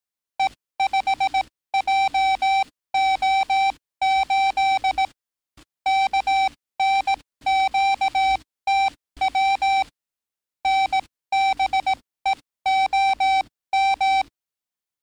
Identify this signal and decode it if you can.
Morse code, heard as a keyed tone, about 18 wpm, spaced as for 15 wpm, 770 Hz. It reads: E5JO8 KNQTW NBEOM
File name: training / NAVPERS 11049to11060 / NAVPERS 11056 RA RB